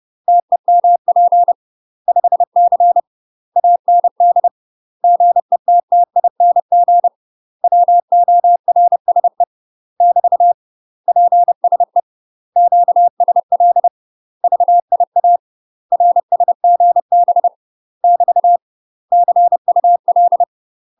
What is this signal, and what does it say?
Morse code, heard as a keyed tone, about 30 wpm, 700 Hz. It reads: TEMP 5C AND GETTING WORSE <BT> PSE QSL VIA RSGB <BT> CUL
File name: UK MASTER QSOs TRAINING / 30WPM / UK MASTER QSOs 0045